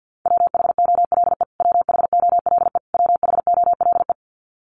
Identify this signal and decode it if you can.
Morse code, heard as a keyed tone, about 50 wpm, 700 Hz. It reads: WHOLE WHOLE WHOLE
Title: Top 500 Words Repeated 50wpm